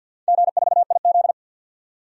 Morse code, heard as a keyed tone, about 47 wpm, 700 Hz. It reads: G4IZ
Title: DXCC-100 Call Signs - 4 Characters - 50wpm